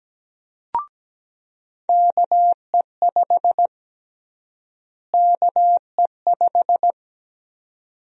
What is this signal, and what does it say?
Morse code, heard as a keyed tone, about 17 wpm, 700 Hz. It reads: KE5 KE5
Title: USA Call Sign Prefixes - Repeated 2x - 17wpm